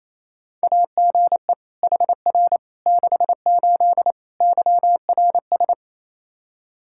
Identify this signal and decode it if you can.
Morse code, heard as a keyed tone, about 28 wpm, 700 Hz. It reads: AGE HR 68 YRS
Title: Practice QSO - Exchange 3 - More Info - 28wpm